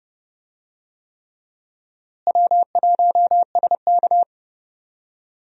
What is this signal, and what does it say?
Morse code, heard as a keyed tone, about 30 wpm, 700 Hz. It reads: W1SK
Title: USA Call Signs - 4 Characters - Repeated 3x - 30wpm